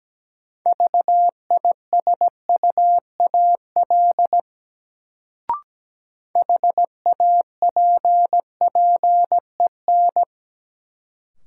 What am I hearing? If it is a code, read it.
Morse code, heard as a keyed tone, about 17 wpm, 700 Hz. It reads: VISUAL HAPPEN
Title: Sets of 5 Words Encore - New Word Drill - 17wpm